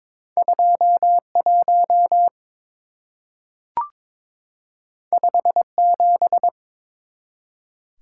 Morse code, heard as a keyed tone, about 22 wpm, 700 Hz. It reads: 21 57
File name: practice / 22wpm / Two Letter-Number Combinations - 22wpm